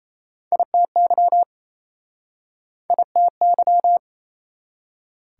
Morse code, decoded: ITY ITY

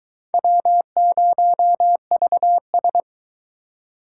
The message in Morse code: W0VS

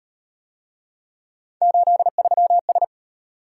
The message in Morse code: 83S

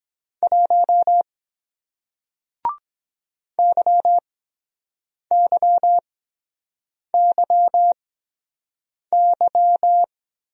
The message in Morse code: 1 YYYY

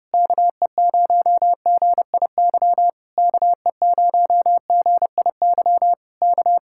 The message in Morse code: KE0GIY KE0GIY K

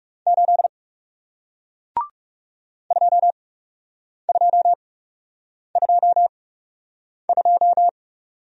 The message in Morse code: 8 2222

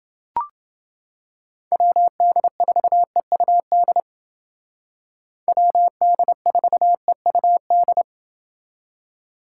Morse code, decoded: WD4EUD WD4EUD

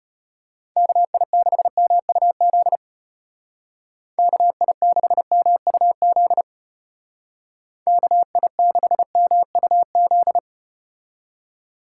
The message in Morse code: KI6MUZ KI6MUZ KI6MUZ

